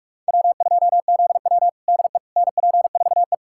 Morse code, heard as a keyed tone, about 41 wpm, 700 Hz. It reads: W2ZW DE NP4E
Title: USA Callsign DE Callsign - 4 Characters - 45wpm